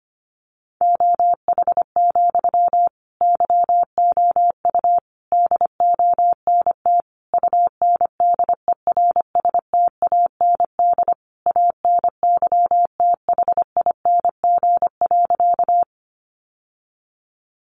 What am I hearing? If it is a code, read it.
Morse code, heard as a keyed tone, about 25 wpm, 700 Hz. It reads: OH, YOU DONT UNDERSTAND ANYTHING.
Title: Sentences from Top 1000 Words 25wpm